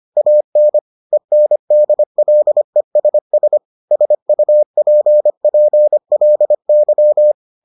Morse code, heard as a keyed tone, about 25 wpm, 600 Hz. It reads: AN ENDLESS SUPPLY